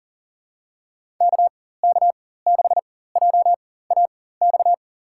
Morse code, heard as a keyed tone, about 40 wpm, spaced as for 19 wpm, 700 Hz. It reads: KK6JAX
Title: USA Call Signs - 6 Characters - 4x Character Spacing - 40wpm